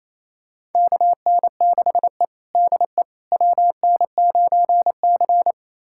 Morse code, heard as a keyed tone, about 28 wpm, 700 Hz. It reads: KN6E DE WN9C
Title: USA Callsign DE Callsign - 4 Characters - 28wpm